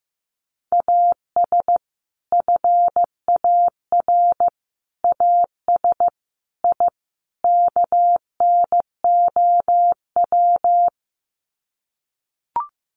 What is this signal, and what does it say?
Morse code, heard as a keyed tone, about 15 wpm, 700 Hz. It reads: AS FAR AS I KNOW
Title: Sets of 5 Words 15wpm